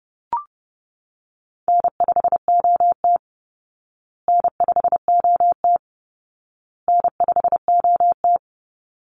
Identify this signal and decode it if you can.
Morse code, heard as a keyed tone, about 30 wpm, 700 Hz. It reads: N5OT N5OT N5OT
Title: Easy Call Signs Repeated 3x - 30wpm